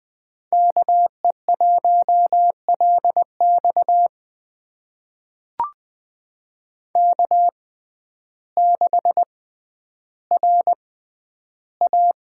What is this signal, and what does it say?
Morse code, heard as a keyed tone, about 20 wpm, spaced as for 11 wpm, 700 Hz. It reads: KE1LX K6RA
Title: USA Call Signs - 6x Character Spacing - 20wpm